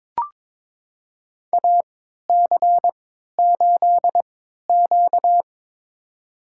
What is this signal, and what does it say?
Morse code, heard as a keyed tone, about 22 wpm, spaced as for 13 wpm, 700 Hz. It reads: AC8Q
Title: USA Call Signs - 4 Characters - 3x Character Spacing - 22wpm